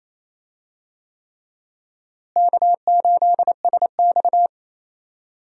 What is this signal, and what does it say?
Morse code, heard as a keyed tone, about 28 wpm, 700 Hz. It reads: K8SX
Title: USA Call Signs - 4 Characters - Repeated 3x - 28wpm